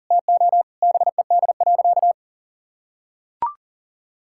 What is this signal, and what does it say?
Morse code, heard as a keyed tone, about 40 wpm, 700 Hz. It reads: TO BED.
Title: Sentences from Top 1000 Words - Review Part 6 - 40wpm